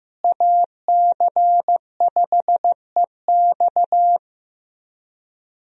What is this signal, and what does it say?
Morse code, heard as a keyed tone, about 16 wpm, 700 Hz. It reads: AC5EX
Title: USA Call Signs - 5 Characters - Repeated 2x - 15wpm